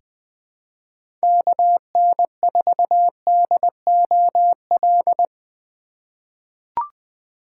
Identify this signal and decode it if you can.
Morse code, heard as a keyed tone, about 20 wpm, 700 Hz. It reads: KN4DOL
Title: USA Call Signs - 6 Characters - 3x Character Spacing - 20wpm